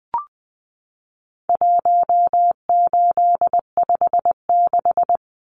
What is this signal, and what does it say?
Morse code, heard as a keyed tone, about 20 wpm, 700 Hz. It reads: 1856
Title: Numbers - 4-Digit 20wpm